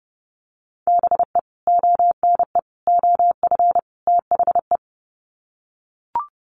Morse code, heard as a keyed tone, about 30 wpm, 700 Hz. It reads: BE ONE OF THE
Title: Sets of 4 Words 30wpm